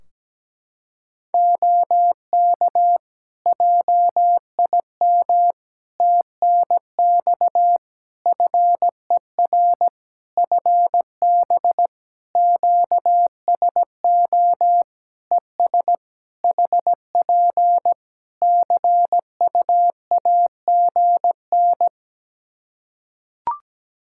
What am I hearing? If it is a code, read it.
Morse code, heard as a keyed tone, about 17 wpm, 700 Hz. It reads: OK JIM TNX FER FB QSO ES HP CUAGN
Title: Practice QSO - Exchange 4 - Ending - 17wpm